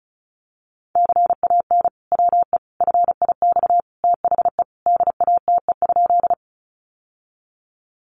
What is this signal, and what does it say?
Morse code, heard as a keyed tone, about 34 wpm, 700 Hz. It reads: CAN WE FIX THE DATE?
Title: Sentences from Top 1400 Words 35wpm